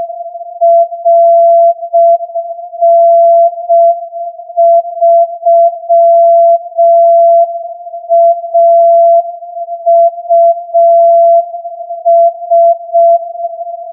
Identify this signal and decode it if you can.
Morse code, heard as a keyed tone, about 6 wpm, 680 Hz. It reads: RN3AUS
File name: 8feb AGC CW